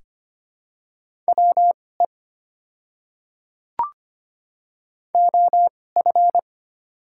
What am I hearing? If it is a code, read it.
Morse code, heard as a keyed tone, about 25 wpm, spaced as for 18 wpm, 690 Hz. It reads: WE OF
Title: 2x Character Spacing - Top 100 Words - 25wpm